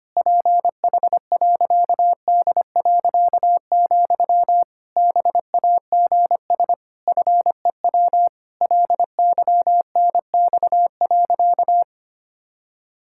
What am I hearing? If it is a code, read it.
Morse code, heard as a keyed tone, about 25 wpm, 700 Hz. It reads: PH.D., BAGS FEW LYNX.
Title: Pangram Challenge v2 - 25wpm